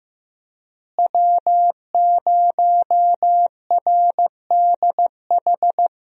Morse code, heard as a keyed tone, about 15 wpm, 700 Hz. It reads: W0RDH
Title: USA Call Signs - 5 Characters - Repeated 2x - 15wpm